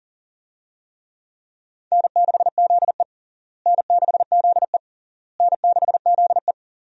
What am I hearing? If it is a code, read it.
Morse code, heard as a keyed tone, about 40 wpm, 700 Hz. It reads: N6ZE N6ZE N6ZE
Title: USA Call Signs - 4 Characters - Repeated 3x - 40wpm